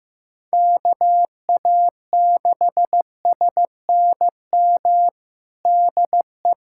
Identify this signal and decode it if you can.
Morse code, heard as a keyed tone, about 15 wpm, 700 Hz. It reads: KA6SNM DE
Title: USA Callsign DE Callsign - 6 Characters - 15wpm